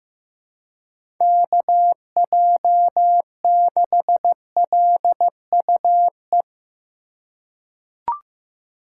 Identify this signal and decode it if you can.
Morse code, heard as a keyed tone, about 15 wpm, 700 Hz. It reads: KJ6LUE
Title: USA Call Signs - 2x Character Spacing - 15wpm